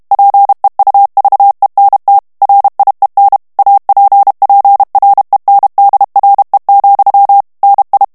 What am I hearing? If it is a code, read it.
Morse code, heard as a keyed tone, about 32 wpm, 800 Hz. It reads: PEUVENT RIEN APPRENDRE, NI
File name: FRANCAIS TEXTE CLAIR 2 VITESSE 30 MOTS MINUTES